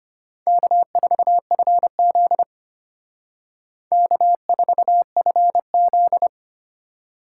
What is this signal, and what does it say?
Morse code, heard as a keyed tone, about 32 wpm, 700 Hz. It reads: K4FZ K4FZ